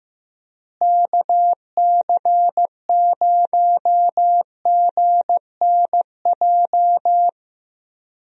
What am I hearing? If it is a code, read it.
Morse code, heard as a keyed tone, about 15 wpm, 700 Hz. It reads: KC0GNJ
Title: USA Call Signs - 6 Characters - Repeated 3x - 15wpm